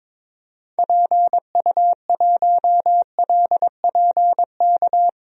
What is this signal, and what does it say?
Morse code, heard as a keyed tone, about 21 wpm, 700 Hz. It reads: PU1LPK
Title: DXCC-100 Call Signs - 6 Characters - 22wpm